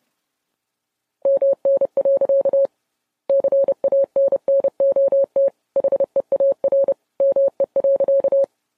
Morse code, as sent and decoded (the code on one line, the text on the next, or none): -- -. .-.-.- / -.-. .- -. -. --- - / .... .- .-. / -- . .-.-.-
MN. CANNOT HEAR ME.